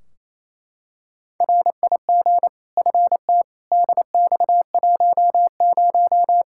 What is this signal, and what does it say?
Morse code, heard as a keyed tone, about 28 wpm, 700 Hz. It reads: RIG FT DX10
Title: Practice QSO - Exchange 2 - Info - 28wpm